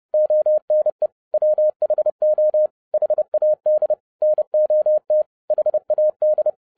Morse code, heard as a keyed tone, about 30 wpm, 610 Hz. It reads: ONE WHO HAD NOT HAD
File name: 2011-07-mp3 / n55 BACK FROM A BIG ACCIDENT 30wpm